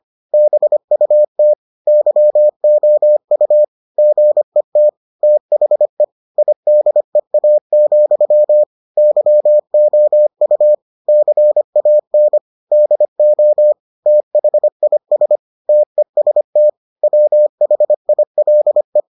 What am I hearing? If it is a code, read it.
Morse code, heard as a keyed tone, about 25 wpm, 600 Hz. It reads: BUT YOU GET THE IDEA, YOU CAN DO THIS TEST WHILE